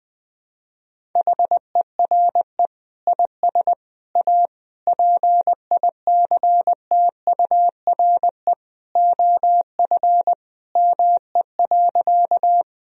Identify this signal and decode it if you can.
Morse code, heard as a keyed tone, about 20 wpm, 700 Hz. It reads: HERE IS A PICTURE OF ME.
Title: Sentences from Top 1000 Words - Review Part 1 - 20wpm